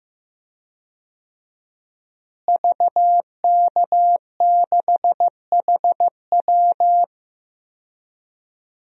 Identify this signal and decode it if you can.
Morse code, heard as a keyed tone, about 15 wpm, 700 Hz. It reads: VK6HW
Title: DXCC-100 Call Signs - 5 Characters - Repeated 2x - 15wpm